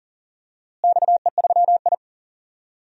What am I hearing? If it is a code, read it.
Morse code, heard as a keyed tone, about 40 wpm, 700 Hz. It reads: XE3I
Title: DXCC-100 Call Signs - 4 Characters - Repeated 2x - 40wpm